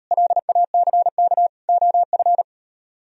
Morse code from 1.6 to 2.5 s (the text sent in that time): OF